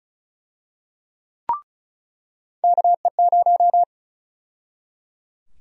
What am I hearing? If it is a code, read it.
Morse code, heard as a keyed tone, about 34 wpm, 700 Hz. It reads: KE0
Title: USA Call Sign Prefixes - 35wpm